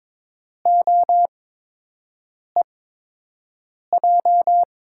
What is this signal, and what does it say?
Morse code, heard as a keyed tone, about 22 wpm, spaced as for 6 wpm, 700 Hz. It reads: OEJ
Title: USA Call Signs - 6 Characters - 8x Character Spacing - 22wpm